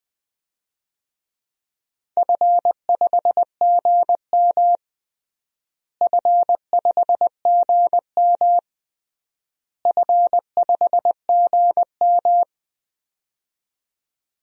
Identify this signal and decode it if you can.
Morse code, heard as a keyed tone, about 21 wpm, 700 Hz. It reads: F5GM F5GM F5GM
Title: DXCC-100 Call Signs - 4 Characters - Repeated 3x - 20wpm